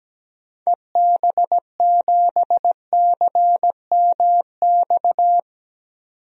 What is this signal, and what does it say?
Morse code, heard as a keyed tone, about 17 wpm, 700 Hz. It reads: EB7CMX
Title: DXCC-100 Call Signs - 6 Characters - 17wpm